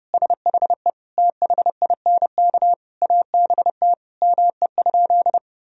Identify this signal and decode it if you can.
Morse code, heard as a keyed tone, about 30 wpm, 700 Hz. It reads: SHE THINK ABT ME?
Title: Sentences from Top 200 Words and Abbreviations 30wpm